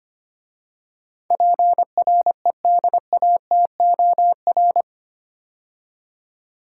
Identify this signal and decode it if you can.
Morse code, heard as a keyed tone, about 25 wpm, 700 Hz. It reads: PREDATOR
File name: 8 Letter Words - First 3 Letters Given - 25wpm